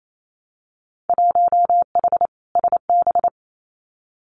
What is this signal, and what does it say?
Morse code, heard as a keyed tone, about 28 wpm, 700 Hz. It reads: 1H SB
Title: Winter Field Day Exchanges 28wpm